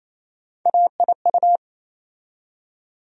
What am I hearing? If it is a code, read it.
Morse code, heard as a keyed tone, about 28 wpm, 700 Hz. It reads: AIU
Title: USA Call Sign Postfixes - 3 Letters 28wpm